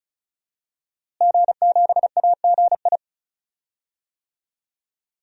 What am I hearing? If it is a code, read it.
Morse code, heard as a keyed tone, about 34 wpm, 700 Hz. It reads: G7AGI
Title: DXCC-100 Call Signs - 5 Characters - Repeated 3x - 35wpm